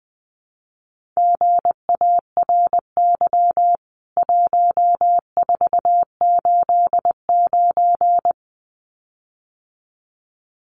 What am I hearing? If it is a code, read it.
Morse code, heard as a keyed tone, about 20 wpm, 700 Hz. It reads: GARY 1489
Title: CWT Exchange 20wpm